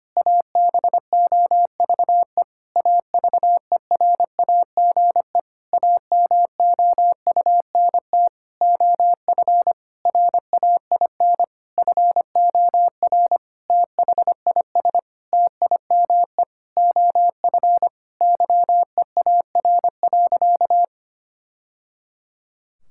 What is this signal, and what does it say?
Morse code, heard as a keyed tone, about 25 wpm, 700 Hz. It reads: ABOVE AVERAGE AMOUNT OF RAIN FOR THIS TIME OF YEAR.
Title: Sentences from Top 1000 Words - Review Part 6 - 25wpm